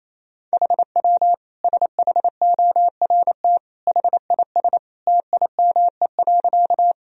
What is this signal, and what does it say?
Morse code, heard as a keyed tone, about 27 wpm, 700 Hz. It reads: HW SHORT HIS TIME.
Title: Sentences from Top 400 Words and Abbreviations 28wpm